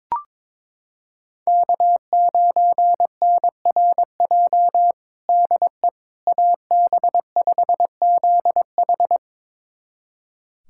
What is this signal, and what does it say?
Morse code, heard as a keyed tone, about 22 wpm, 700 Hz. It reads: K9NRJ DE AB5ZH
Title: USA Callsign DE Callsign - 5 Characters - 22wpm